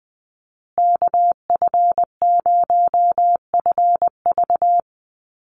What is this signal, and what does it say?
Morse code, heard as a keyed tone, about 19 wpm, 700 Hz. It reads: KF0FV
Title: USA Call Signs Repeated 3x 20wpm